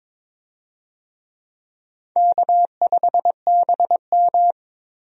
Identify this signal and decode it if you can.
Morse code, heard as a keyed tone, about 22 wpm, 700 Hz. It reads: K5BM
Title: USA Call Signs - 4 Characters - Repeated 3x - 22wpm